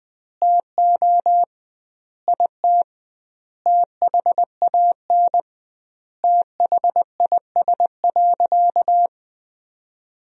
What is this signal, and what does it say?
Morse code, heard as a keyed tone, about 21 wpm, 700 Hz. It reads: TO IT THAN THIS.
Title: 2x Word-Spacing - Sentences from Top 100 Words - 20wpm